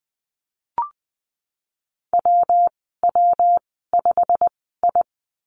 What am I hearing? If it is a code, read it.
Morse code, heard as a keyed tone, about 20 wpm, spaced as for 15 wpm, 700 Hz. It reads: WW5I
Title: 161 Lesson 26g - Letter g - Call Signs 20wpm